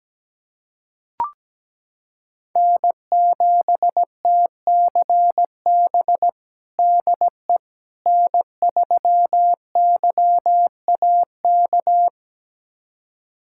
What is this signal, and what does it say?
Morse code, heard as a keyed tone, about 17 wpm, 700 Hz. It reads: N7TCB DE N3YAK
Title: USA Callsign DE Callsign - 5 Characters - 17wpm